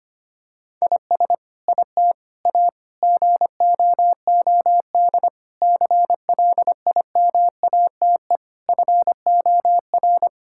Morse code, heard as a keyed tone, about 25 wpm, 700 Hz. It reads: IS IT A GOOD CLIMATE FOR
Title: Sentences from Top 1600 Words 25wpm